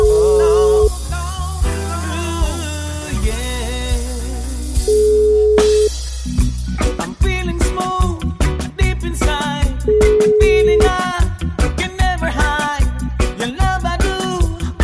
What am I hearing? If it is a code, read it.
Ringback tone (Europe, 425 Hz), with noise.